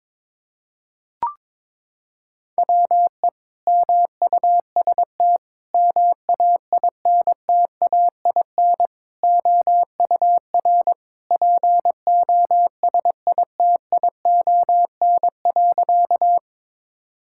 Morse code, heard as a keyed tone, about 22 wpm, 700 Hz. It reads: WE MUST MAINTAIN OUR POSITION.
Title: Sentences from Top 1000 Words - Review Part 5 - 22wpm